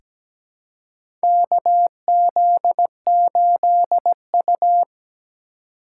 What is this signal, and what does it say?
Morse code, heard as a keyed tone, about 17 wpm, 700 Hz. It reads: KZ8U
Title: USA Call Signs - 4 Characters - 4x Character Spacing - 17wpm